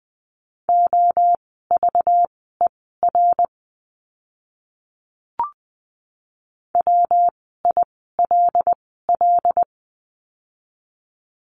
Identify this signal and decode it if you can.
Morse code, heard as a keyed tone, about 20 wpm, spaced as for 15 wpm, 700 Hz. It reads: OVER WILL
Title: 192 Lesson 31f - Letter v - Words 20wpm